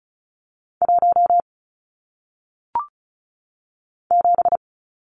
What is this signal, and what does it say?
Morse code, heard as a keyed tone, about 35 wpm, 700 Hz. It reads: 1 7
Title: Numbers - 1-Digit 35wpm